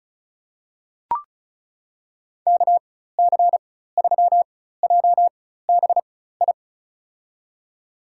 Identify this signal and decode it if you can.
Morse code, heard as a keyed tone, about 36 wpm, spaced as for 16 wpm, 700 Hz. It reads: KC3JBI
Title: USA Call Signs - 6 Characters - 4x Character Spacing - 35wpm